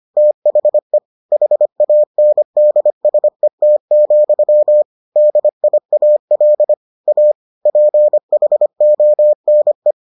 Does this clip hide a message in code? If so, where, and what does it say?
Yes: Morse code from 0.2 to 9.9 s, heard as a keyed tone, about 25 wpm, 600 Hz. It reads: THE HANDSET, DIAL A PHONE